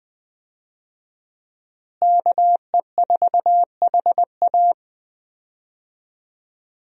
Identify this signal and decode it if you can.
Morse code, heard as a keyed tone, about 21 wpm, 700 Hz. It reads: KE4HA
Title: USA Call Signs - 5 Characters - Repeated 3x - 20wpm